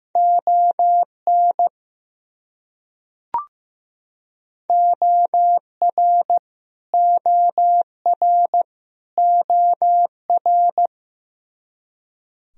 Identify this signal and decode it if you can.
Morse code, heard as a keyed tone, about 15 wpm, 700 Hz. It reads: ON OR OR OR
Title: Top 100 Words Repeated 15wpm